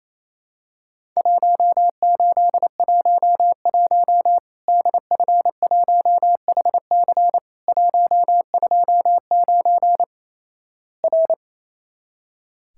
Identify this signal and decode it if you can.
Morse code, heard as a keyed tone, about 28 wpm, 700 Hz. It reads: 1811 DF1HC 129 R
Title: Worked All Europe DX Contest - QTC Exchange - 28wpm